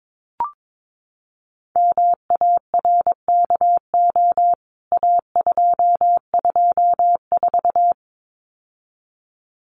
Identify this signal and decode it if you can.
Morse code, heard as a keyed tone, about 22 wpm, 700 Hz. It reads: MARKO A224
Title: CWT Exchange 22wpm